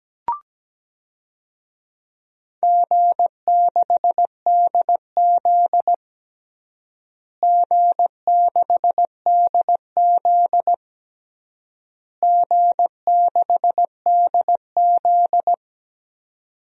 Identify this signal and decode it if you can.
Morse code, heard as a keyed tone, about 17 wpm, 700 Hz. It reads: G6DZ G6DZ G6DZ
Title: DXCC-100 Call Signs - 4 Characters - Repeated 3x - 17wpm